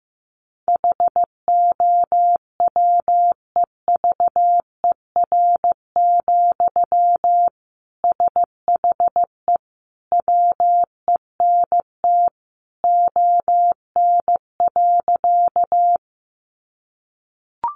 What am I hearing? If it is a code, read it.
Morse code, heard as a keyed tone, about 15 wpm, 700 Hz. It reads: HOWEVER, SHE WENT ON.
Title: Sentences from Top 600 Words 15wpm